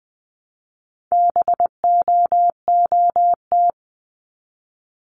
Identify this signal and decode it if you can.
Morse code, heard as a keyed tone, about 20 wpm, 700 Hz. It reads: BOOT
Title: 260 Lesson 42e - Words 20-20wpm